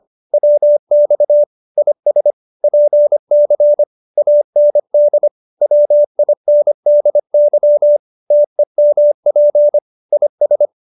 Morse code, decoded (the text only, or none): WX IS PC AND WINDY TEMP IS